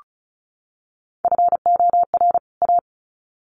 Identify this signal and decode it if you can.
Morse code, heard as a keyed tone, about 34 wpm, 700 Hz. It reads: FOR A
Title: Sets of 2 Words 35wpm